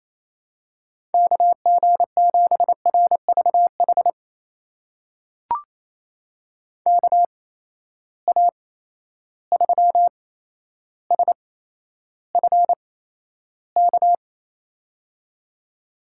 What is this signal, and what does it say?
Morse code, heard as a keyed tone, about 28 wpm, spaced as for 12 wpm, 700 Hz. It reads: KG7RVH KA3SFK
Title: USA Call Signs - 6 Characters - 8x Character Spacing - 28wpm